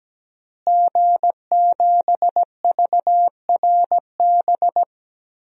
Morse code, heard as a keyed tone, about 17 wpm, 700 Hz. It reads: G7VRB
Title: DXCC-100 Call Signs - 5 Characters - Repeated 3x - 17wpm